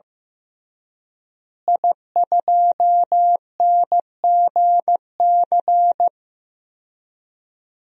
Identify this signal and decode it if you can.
Morse code, heard as a keyed tone, about 15 wpm, 700 Hz. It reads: I2NGC
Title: DXCC-100 Call Signs - 5 Characters - Repeated 3x - 15wpm